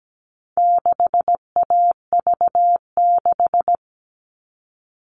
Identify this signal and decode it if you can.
Morse code, heard as a keyed tone, about 17 wpm, 700 Hz. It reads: 6AV6
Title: All Cut Numbers - 4-Digit 17wpm